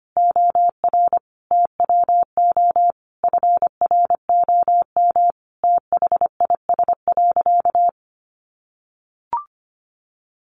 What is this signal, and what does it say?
Morse code, heard as a keyed tone, about 25 wpm, 700 Hz. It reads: OR TWO FROM THIS.
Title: Sentences from Top 700 Words 25wpm